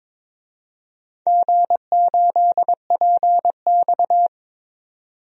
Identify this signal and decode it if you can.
Morse code, heard as a keyed tone, about 22 wpm, 700 Hz. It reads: G8PX
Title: DXCC-100 Call Signs - 4 Characters - 22wpm